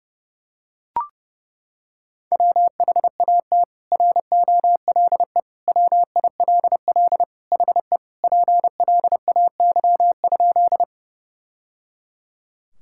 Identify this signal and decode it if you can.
Morse code, heard as a keyed tone, about 30 wpm, 700 Hz. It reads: WHAT ROLE WILL HE PLAY?
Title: Sentences from Top 1000 Words - Review Part 6 - 30wpm